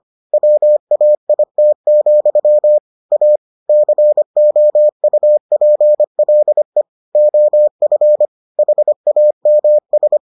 Morse code, heard as a keyed tone, about 25 wpm, 600 Hz. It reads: WAIT, A COUPLE OF HAMS